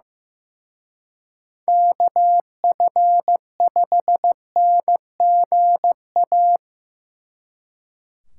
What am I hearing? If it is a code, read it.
Morse code, heard as a keyed tone, about 17 wpm, 700 Hz. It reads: KF5NGA
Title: Speed Racing - USA 2x3 Call Signs 15wpm